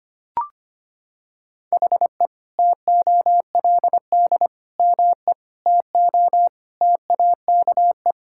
Morse code, heard as a keyed tone, about 25 wpm, 700 Hz. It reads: HE TOLD ME TO TAKE
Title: Sentences from Top 1000 Words - Review Part 6 - 25wpm